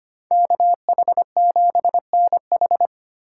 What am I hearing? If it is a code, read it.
Morse code, heard as a keyed tone, about 25 wpm, 700 Hz. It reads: KH7NH
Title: USA Call Signs - 2x Character Spacing - 25wpm